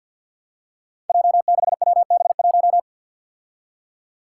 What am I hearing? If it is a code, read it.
Morse code, heard as a keyed tone, about 47 wpm, 700 Hz. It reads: J6WB1
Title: Five Challenging Letter-Number Combinations 50wpm